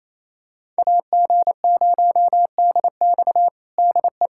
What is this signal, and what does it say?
Morse code, heard as a keyed tone, about 28 wpm, 700 Hz. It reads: AG0DX DE
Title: USA Callsign DE Callsign - 5 Characters - 28wpm